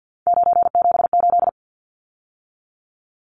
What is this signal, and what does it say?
Morse code, heard as a keyed tone, about 50 wpm, 700 Hz. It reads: 978
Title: Numbers - 3-Digit 50wpm